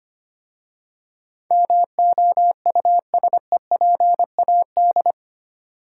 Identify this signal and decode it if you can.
Morse code, heard as a keyed tone, about 27 wpm, 700 Hz. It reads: MOUSEPAD
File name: Easy Words - Speed-Racing with Hints - 25wpm